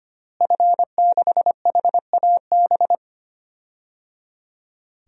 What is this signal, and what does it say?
Morse code, heard as a keyed tone, about 25 wpm, 700 Hz. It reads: F6HAB
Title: DXCC-100 Call Signs - 5 Characters - Repeated 3x - 25wpm